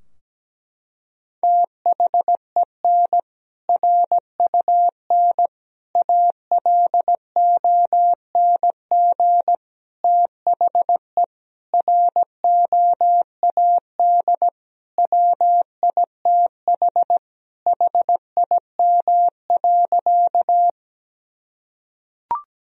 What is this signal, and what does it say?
Morse code, heard as a keyed tone, about 17 wpm, 700 Hz. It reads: THEN RUN ALONG THE ROAD WITH HIM.
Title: Sentences from Top 1000 Words - Review Part 2 - 17wpm